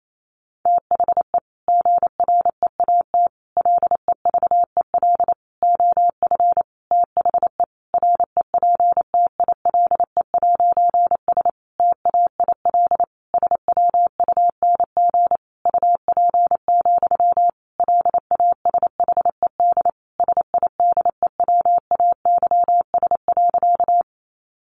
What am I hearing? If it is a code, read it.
Morse code, heard as a keyed tone, about 28 wpm, 700 Hz. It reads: THE GREAT LEVEL OF THE REPTILE'S TAIL SWUNG UP, LASHED SIDEWAYS.
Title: A Sound Of Thunder 28wpm